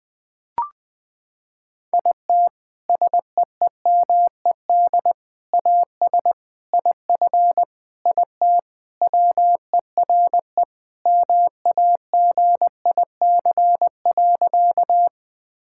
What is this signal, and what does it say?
Morse code, heard as a keyed tone, about 20 wpm, 700 Hz. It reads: IT SEEMED AS IF IT WERE MAGIC.
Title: Sentences with 4-Word Phrases - 20wpm